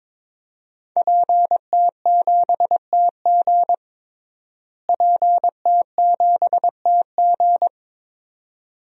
Morse code, heard as a keyed tone, about 23 wpm, 700 Hz. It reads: PT7TG PT7TG
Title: DXCC-100 Call Signs - 5 Characters - Repeated 2x - 22wpm